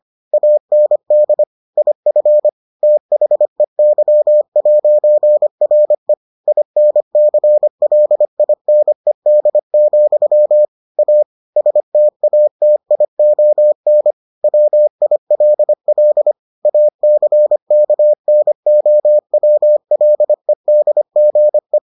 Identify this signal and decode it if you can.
Morse code, heard as a keyed tone, about 25 wpm, 600 Hz. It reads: AND IF THEY'RE INCLINED, A STATION WILL ACKNOWLEDGE